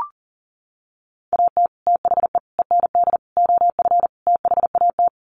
Morse code, heard as a keyed tone, about 40 wpm, 700 Hz. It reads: AT THE END OF THAT
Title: Sets of 5 Words Encore2 40wpm